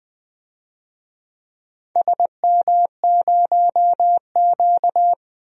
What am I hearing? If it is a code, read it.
Morse code, heard as a keyed tone, about 20 wpm, 690 Hz. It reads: SM0Q
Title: DXCC-100 Call Signs - 4 Characters - Repeated 3x - 20wpm